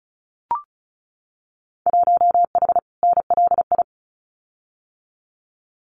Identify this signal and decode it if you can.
Morse code, heard as a keyed tone, about 34 wpm, 700 Hz. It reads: 1H NLI
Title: Winter Field Day Exchanges 35wpm